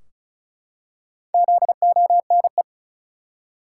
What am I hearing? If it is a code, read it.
Morse code, heard as a keyed tone, about 36 wpm, 700 Hz. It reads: ZONE
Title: Four Letter Words - 35wpm